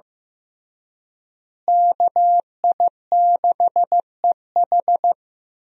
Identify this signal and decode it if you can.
Morse code, heard as a keyed tone, about 15 wpm, 700 Hz. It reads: KI6EH
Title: USA Call Signs - 5 Characters - Repeated 3x - 15wpm